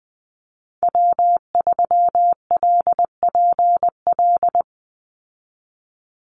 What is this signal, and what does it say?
Morse code, heard as a keyed tone, about 20 wpm, 700 Hz. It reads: W3LPL
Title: Call Signs Repeated 3x - 20wpm